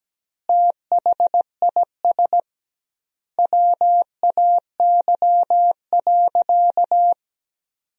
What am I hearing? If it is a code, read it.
Morse code, heard as a keyed tone, about 17 wpm, 700 Hz. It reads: THIS WAY.
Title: 2x Word-Spacing - Sentences from Top 100 Words - 17wpm